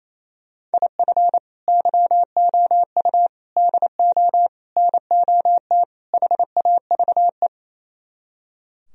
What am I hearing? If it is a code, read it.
Morse code, heard as a keyed tone, about 36 wpm, 700 Hz. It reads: IF YOU DO NOT HAVE